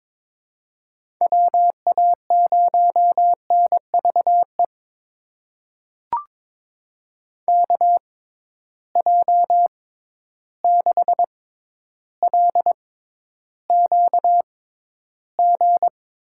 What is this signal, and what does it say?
Morse code, heard as a keyed tone, about 22 wpm, spaced as for 11 wpm, 700 Hz. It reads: WA0NVE KJ6LQG
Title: USA Call Signs - 6 Characters - 6x Character Spacing - 22wpm